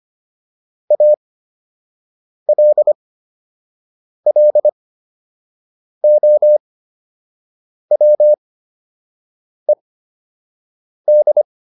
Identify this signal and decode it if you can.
Morse code, heard as a keyed tone, about 25 wpm, spaced as for 6 wpm, 600 Hz. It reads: ALLOWED
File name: ED8 06